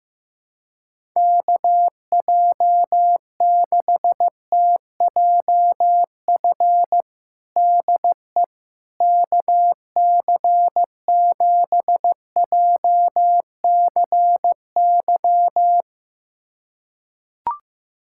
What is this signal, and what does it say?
Morse code, heard as a keyed tone, about 16 wpm, 700 Hz. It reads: KJ6TJF DE KC7JCY
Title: USA Callsign DE Callsign - 6 Characters - 15wpm